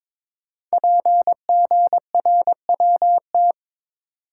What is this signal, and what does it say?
Morse code, heard as a keyed tone, about 22 wpm, 700 Hz. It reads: PGRWT
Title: Code Groups - 5 Letters - 22wpm